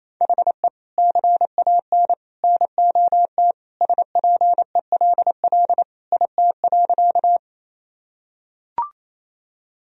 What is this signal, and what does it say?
Morse code, heard as a keyed tone, about 28 wpm, 700 Hz. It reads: HE CAN NOT SPELL IT.
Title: Sentences from Top 1000 Words - Review Part 1 - 28wpm